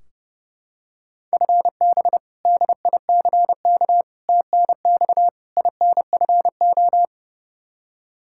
Morse code, heard as a keyed tone, about 30 wpm, 700 Hz. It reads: FB DICK TNX INFO
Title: Practice QSO - Exchange 3 - More Info - 30wpm